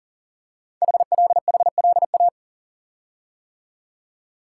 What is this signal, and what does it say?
Morse code, heard as a keyed tone, about 40 wpm, 700 Hz. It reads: HLHLA
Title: Code Groups - 5 Letters - 40wpm